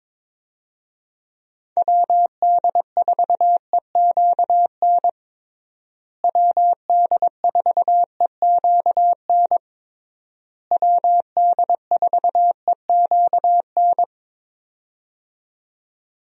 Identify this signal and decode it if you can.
Morse code, heard as a keyed tone, about 22 wpm, 700 Hz. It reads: WD4EQN WD4EQN WD4EQN
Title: USA Call Signs - 6 Characters - Repeated 3x - 22wpm